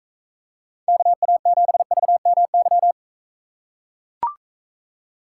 Morse code, heard as a keyed tone, about 42 wpm, 700 Hz. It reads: KA7VMY